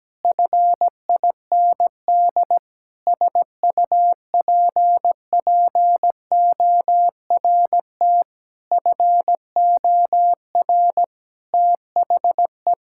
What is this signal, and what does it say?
Morse code, heard as a keyed tone, about 17 wpm, 700 Hz. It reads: FIND SUPPORT FOR THE
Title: Sentences from Top 1000 Words - Review Part 5 - 17wpm